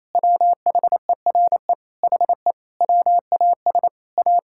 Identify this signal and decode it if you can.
Morse code, heard as a keyed tone, about 28 wpm, 700 Hz. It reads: WHERE HE WAS A
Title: Sentences from Top 1000 Words - Review Part 5 - 28wpm